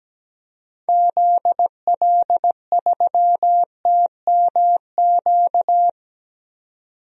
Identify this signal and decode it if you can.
Morse code, heard as a keyed tone, about 17 wpm, 700 Hz. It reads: ZL3TMQ
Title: DXCC-100 Call Signs - 6 Characters - 17wpm